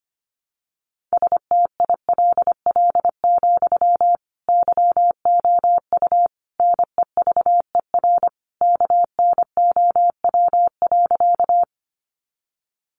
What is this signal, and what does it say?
Morse code, heard as a keyed tone, about 25 wpm, 700 Hz. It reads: STILL, YOU NEVER KNOW.
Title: Sentences from Top 300 Words 25wpm